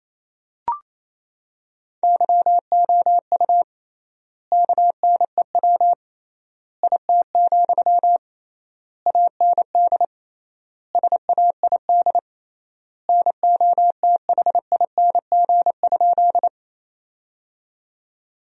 Morse code, heard as a keyed tone, about 28 wpm, 700 Hz. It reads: YOU KNEW IT, AND SAID NOTHING?
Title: 3x Word-Spacing - Sentences from Top 500 Words - 28wpm